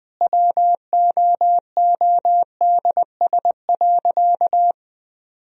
Morse code, heard as a keyed tone, about 21 wpm, 700 Hz. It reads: WOODS.
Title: Sentences from Top 1000 Words - Review Part 2 - 20wpm